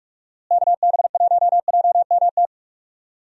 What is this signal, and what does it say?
Morse code, heard as a keyed tone, about 42 wpm, 700 Hz. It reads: KB1JMT
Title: USA Call Signs - 6 Characters - Repeated 3x - 45wpm